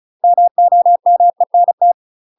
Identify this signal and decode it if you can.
Morse code, heard as a keyed tone, about 35 wpm, 700 Hz. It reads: MOMENT